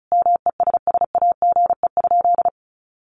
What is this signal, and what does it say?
Morse code, heard as a keyed tone, about 34 wpm, 700 Hz. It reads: MESSAGE?
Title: Sentences from Top 1100 Words 35wpm